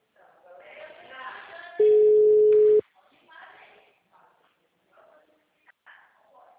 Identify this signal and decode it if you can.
Ringback tone (Europe, 425 Hz), with speech.
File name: out- -2011- - - .62568